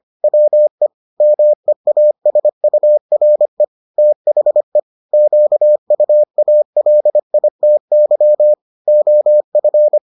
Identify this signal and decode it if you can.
Morse code, heard as a keyed tone, about 25 wpm, 600 Hz. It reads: WE MEASURE THE QUALITY OF